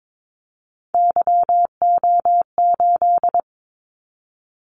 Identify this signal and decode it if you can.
Morse code, heard as a keyed tone, about 22 wpm, 700 Hz. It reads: YO8
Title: Callsign Prefixes for Top 25 DXCC - 22wpm